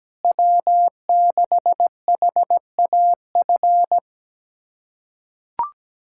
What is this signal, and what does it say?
Morse code, heard as a keyed tone, about 17 wpm, 700 Hz. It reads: W6HAF
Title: USA Callsign DE Callsign - 5 Characters - 17wpm